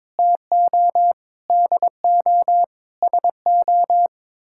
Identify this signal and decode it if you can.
Morse code, heard as a keyed tone, about 22 wpm, 700 Hz. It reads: TO DO SO
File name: practice / 22wpm / Sets of 6 Words - 22wpm